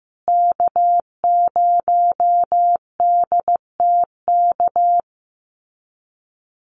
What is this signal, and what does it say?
Morse code, heard as a keyed tone, about 15 wpm, 700 Hz. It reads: K0DTK
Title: USA Call Signs Repeated 3x 15wpm